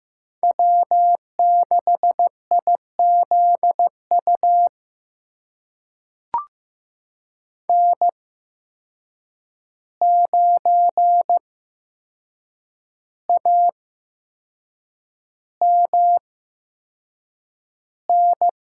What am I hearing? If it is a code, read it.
Morse code, heard as a keyed tone, about 15 wpm, spaced as for 6 wpm, 700 Hz. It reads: W6IZU N9AMN
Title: USA Call Signs - 5 Characters - 8x Character Spacing - 15wpm